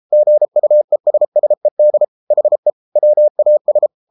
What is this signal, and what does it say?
Morse code, heard as a keyed tone, about 33 wpm, 600 Hz. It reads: GUESSED HE WAS